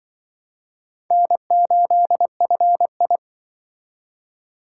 Morse code, heard as a keyed tone, about 25 wpm, 700 Hz. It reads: N8FI
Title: Speed Racing - USA 1x2 Call Signs 20wpm